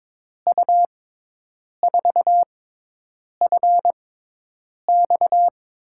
Morse code, heard as a keyed tone, about 22 wpm, spaced as for 8 wpm, 700 Hz. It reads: U4FX
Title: USA Call Signs - 5 Characters - 6x Character Spacing - 22wpm